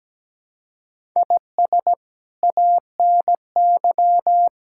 Morse code, heard as a keyed tone, about 17 wpm, 700 Hz. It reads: IS ANY
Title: Sentences from Top 1000 Words - Review Part 6 - 17wpm